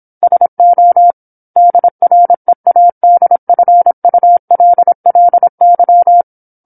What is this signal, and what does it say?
Morse code, heard as a keyed tone, about 27 wpm, 700 Hz. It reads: SO DREADFULLY
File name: 08-Chapter-08 26 WPM